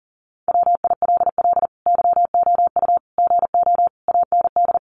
Morse code, heard as a keyed tone, about 40 wpm, 700 Hz. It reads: WILL YOU GO AND
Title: Sentences from Top 1300 Words 40wpm